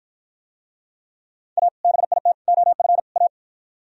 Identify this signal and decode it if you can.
Morse code, heard as a keyed tone, about 53 wpm, 700 Hz. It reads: A BIT OF A